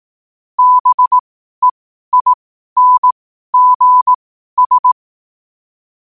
Morse code, heard as a keyed tone, about 18 wpm, spaced as for 12 wpm, 1000 Hz. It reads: BEINGS
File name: qotdfiles / 12wpm